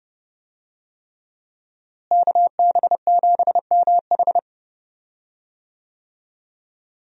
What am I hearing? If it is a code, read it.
Morse code, heard as a keyed tone, about 30 wpm, 700 Hz. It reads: KB7MH
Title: USA Call Signs - 5 Characters - Repeated 3x - 30wpm